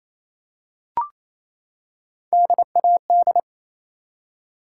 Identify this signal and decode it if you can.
Morse code, heard as a keyed tone, about 27 wpm, 700 Hz. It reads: DAD
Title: Three Letter Words - 28wpm